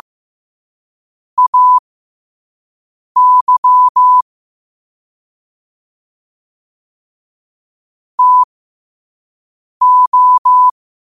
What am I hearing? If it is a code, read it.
Morse code, heard as a keyed tone, about 15 wpm, spaced as for 5 wpm, 1000 Hz. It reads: AY TO
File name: qotdfiles / 5wpm